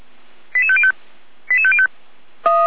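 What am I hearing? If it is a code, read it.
Five-tone selective call (ZVEI), heard as quick runs of five tones, tones of about 70 ms. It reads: 88505 88505
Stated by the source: ZVEI five-tone selective call received with an SDR.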